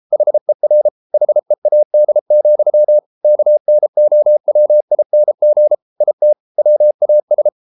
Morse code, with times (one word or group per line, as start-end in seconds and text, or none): 0.1-0.9 HER
1.1-3.0 HEAD,
3.2-5.8 KNOWING
6.0-6.3 IT
6.6-7.5 WAS